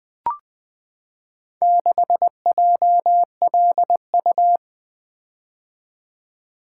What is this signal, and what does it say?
Morse code, heard as a keyed tone, about 20 wpm, 700 Hz. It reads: 6JLU
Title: Four Letter-Number Combinations - 20wpm